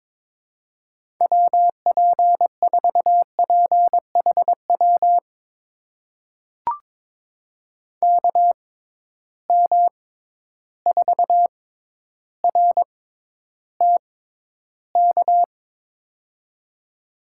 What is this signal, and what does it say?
Morse code, heard as a keyed tone, about 22 wpm, spaced as for 11 wpm, 700 Hz. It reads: WP4PHW KM4RTK
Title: USA Call Signs - 6 Characters - 6x Character Spacing - 22wpm